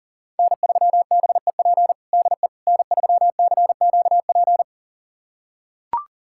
Morse code, heard as a keyed tone, about 37 wpm, 700 Hz. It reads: N3BEP DE N3CQP
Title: USA Callsign DE Callsign - 5 Characters - 40wpm